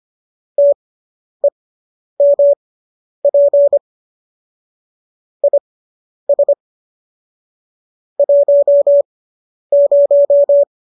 Morse code, heard as a keyed tone, about 25 wpm, spaced as for 10 wpm, 570 Hz. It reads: TEMP IS 10